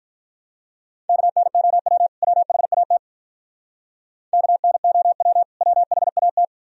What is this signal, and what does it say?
Morse code, heard as a keyed tone, about 43 wpm, 700 Hz. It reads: KNOW WHAT KNOW WHAT
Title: Speed-Racing - Sets of 2 Words 35wpm